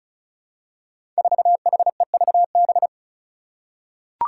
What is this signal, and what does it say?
Morse code, heard as a keyed tone, about 36 wpm, 700 Hz. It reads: 4HEVB
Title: Five Challenging Letter-Number Combinations 35wpm